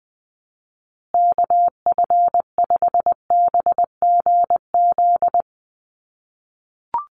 Morse code, heard as a keyed tone, about 20 wpm, 700 Hz. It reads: KF5BGZ
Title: USA Call Signs Repeated 3x 20wpm